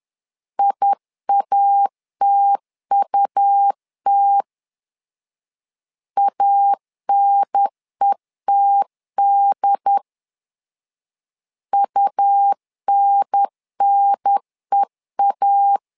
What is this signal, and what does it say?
Morse code, heard as a keyed tone, about 11 wpm, 800 Hz. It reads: IATUT ANETD UNNEA
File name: LEZIONI / 3 MWRO / 3 M 10wpm 1828